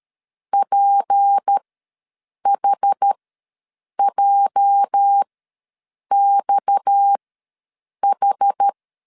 Morse code, heard as a keyed tone, about 13 wpm, spaced as for 7 wpm, 800 Hz. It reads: PHJXH
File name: LEZIONI / 8 YJ45 / 8 YJ45 ws 12wpm 3146